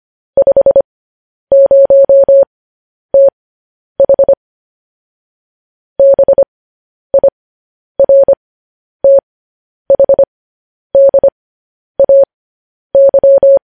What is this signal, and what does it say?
Morse code, heard as a keyed tone, about 25 wpm, spaced as for 10 wpm, 550 Hz. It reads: 50TH BIRTHDAY